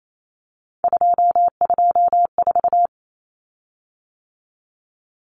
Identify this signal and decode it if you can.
Morse code, heard as a keyed tone, about 28 wpm, 700 Hz. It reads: 224
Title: Common Cut Numbers - 3-Digit 28wpm